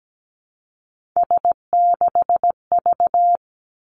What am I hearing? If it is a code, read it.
Morse code, heard as a keyed tone, about 17 wpm, 700 Hz. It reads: S6V
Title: Three Challenging Letter-Number Combinations 17wpm